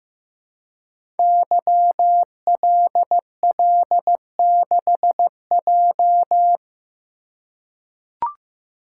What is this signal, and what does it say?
Morse code, heard as a keyed tone, about 15 wpm, 700 Hz. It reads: YLL6J
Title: Code Groups - 5 Letter:Numbers - 15wpm